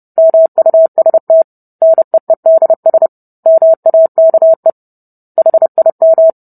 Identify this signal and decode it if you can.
Morse code, heard as a keyed tone, about 30 wpm, 650 Hz. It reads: MUST NEEDS MAKE HIM